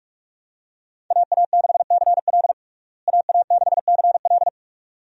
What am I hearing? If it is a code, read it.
Morse code, heard as a keyed tone, about 41 wpm, 700 Hz. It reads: AA6CL AA6CL
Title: USA Call Signs - 5 Characters - Repeated 2x - 45wpm